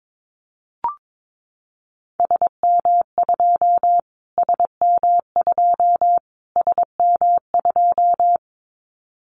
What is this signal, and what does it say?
Morse code, heard as a keyed tone, about 22 wpm, 700 Hz. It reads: SM2 SM2 SM2
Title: Callsign Prefixes for Top 25 DXCC - Repeated 3x - 22wpm